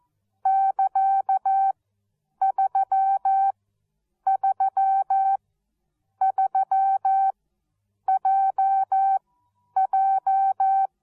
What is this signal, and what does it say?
Morse code, heard as a keyed tone, about 14 wpm, spaced as for 8 wpm, 770 Hz. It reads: <KA>333JJ
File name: Onlinelehrgang / morsen / DJ4UF Morsen13